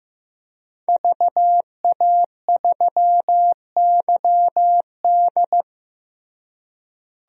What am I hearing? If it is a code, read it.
Morse code, heard as a keyed tone, about 15 wpm, 700 Hz. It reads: VA3YD
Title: DXCC-100 Call Signs - 5 Characters - Repeated 2x - 15wpm